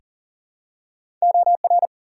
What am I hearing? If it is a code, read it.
Morse code, heard as a keyed tone, about 38 wpm, 700 Hz. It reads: OR
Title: US States and Canadian Provinces 40wpm